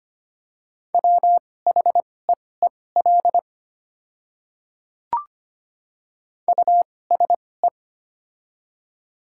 Morse code, heard as a keyed tone, about 25 wpm, spaced as for 18 wpm, 700 Hz. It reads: WHEEL USE
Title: 2x Character Spacing - Top 500 Words - 25wpm